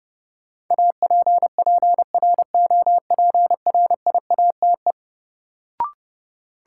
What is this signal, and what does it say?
Morse code, heard as a keyed tone, about 30 wpm, 700 Hz. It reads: APPROPRIATE
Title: Sentences from Top 900 Words - New Word Drill - 30wpm